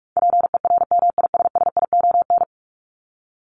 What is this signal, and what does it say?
Morse code, heard as a keyed tone, about 44 wpm, 700 Hz. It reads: PERMISSION
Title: Top 1501-1750 Words 45wpm